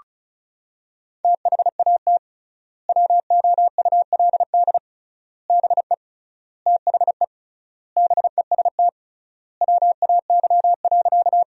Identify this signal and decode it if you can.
Morse code, heard as a keyed tone, about 34 wpm, 700 Hz. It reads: THAT WOULD BE THE BEST WAY.
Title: 3x Word-Spacing - Sentences from Top 400 Words - 35wpm